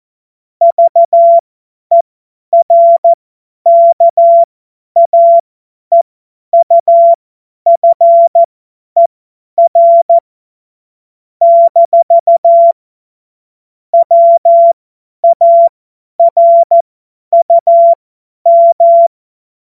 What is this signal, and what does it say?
Morse code, heard as a keyed tone, about 14 wpm, spaced as for 10 wpm, 680 Hz. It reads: VERKAEUFER - WARUM
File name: Morsekurs / Witze / Witz 02. Mai 2023 50 F70